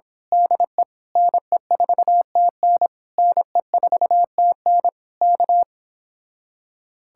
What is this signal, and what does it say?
Morse code, heard as a keyed tone, about 26 wpm, 700 Hz. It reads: DE NE4TN NE4TN K
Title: Speed-Racing Common QSO Phrases Encore - 22wpm